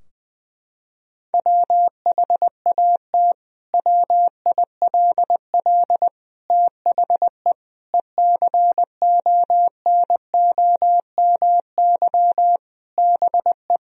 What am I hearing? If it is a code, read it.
Morse code, heard as a keyed tone, about 20 wpm, 700 Hz. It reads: WHAT WILL THE ECONOMY BE
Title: Sentences from Top 1000 Words - Review Part 3 - 20wpm